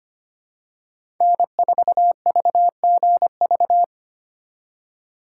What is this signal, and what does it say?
Morse code, heard as a keyed tone, about 25 wpm, 700 Hz. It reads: N4VGV
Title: USA Call Signs - 5 Characters - 3x Character Spacing - 25wpm